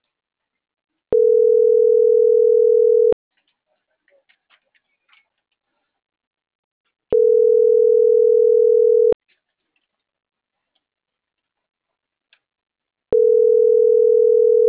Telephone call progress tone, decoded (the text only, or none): ringback tone